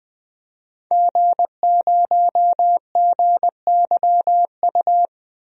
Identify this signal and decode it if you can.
Morse code, heard as a keyed tone, about 20 wpm, 700 Hz. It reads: G0GYU
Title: DXCC-100 Call Signs - 5 Characters - Repeated 3x - 20wpm